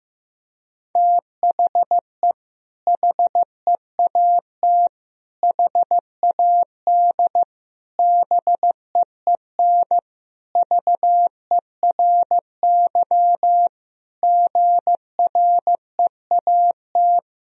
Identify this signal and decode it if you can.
Morse code, heard as a keyed tone, about 16 wpm, 700 Hz. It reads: THE HEAT HAD BEEN VERY GREAT
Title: Sentences from Top 1000 Words - Review Part 3 - 15wpm